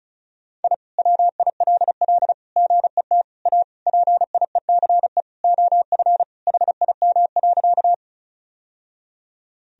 Morse code, heard as a keyed tone, about 34 wpm, 700 Hz. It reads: I WILL GET A PIECE OF HIM.
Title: Sentences from Top 1000 Words - Review Part 2 - 35wpm